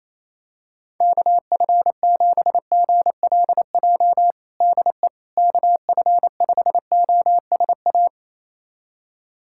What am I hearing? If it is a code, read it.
Morse code, heard as a keyed tone, about 28 wpm, 700 Hz. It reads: KF7GLJ DE KF5OSA
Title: USA Callsign DE Callsign - 6 Characters - 28wpm